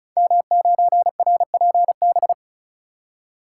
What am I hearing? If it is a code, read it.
Morse code, heard as a keyed tone, about 36 wpm, 690 Hz. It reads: M9RPB
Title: Code Groups - 5 Letter:Numbers - 35wpm